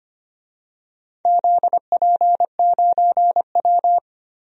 Morse code, heard as a keyed tone, about 25 wpm, 700 Hz. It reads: ZP9W
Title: DXCC-100 Call Signs - 4 Characters - Repeated 3x - 25wpm